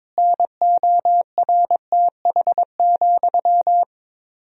Morse code, heard as a keyed tone, about 22 wpm, 700 Hz. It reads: NORTH,
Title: 4x Word-Spacing - Sentences from Top 400 Words - 22wpm